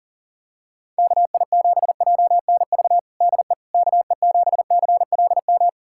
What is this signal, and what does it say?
Morse code, heard as a keyed tone, about 37 wpm, 700 Hz. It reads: KI7JNV DE KE7CLM
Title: USA Callsign DE Callsign - 6 Characters - 40wpm